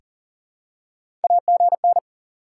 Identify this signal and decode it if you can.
Morse code, heard as a keyed tone, about 40 wpm, 700 Hz. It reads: AGN
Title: SOTA QSO Practice 40wpm